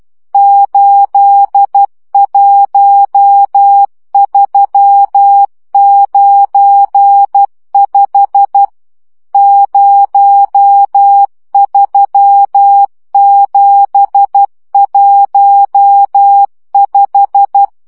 Morse code, heard as a keyed tone, about 12 wpm, 800 Hz. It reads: 81395 03715
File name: morze vizsga 2011-2 10